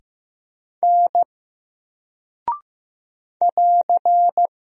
Morse code, heard as a keyed tone, about 15 wpm, 700 Hz. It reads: N <AR>
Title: Single Character (Letter:Number:Punctuation:Prosign) Alt 15wpm